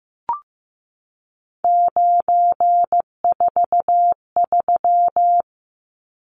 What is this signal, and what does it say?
Morse code, heard as a keyed tone, about 15 wpm, 700 Hz. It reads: 943
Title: Numbers - 3-Digit 15wpm